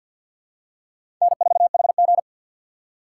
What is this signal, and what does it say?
Morse code, heard as a keyed tone, about 51 wpm, 700 Hz. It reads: N4HG
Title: USA Call Signs - 4 Characters - 3x Character Spacing - 50wpm